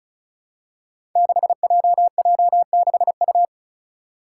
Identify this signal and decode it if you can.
Morse code, heard as a keyed tone, about 34 wpm, 700 Hz. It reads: 6JJ6U
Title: Five Challenging Letter-Number Combinations 35wpm